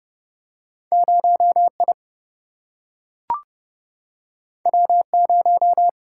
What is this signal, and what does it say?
Morse code, heard as a keyed tone, about 30 wpm, 700 Hz. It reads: 0I W0
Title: Single-Letter Single-Number 30wpm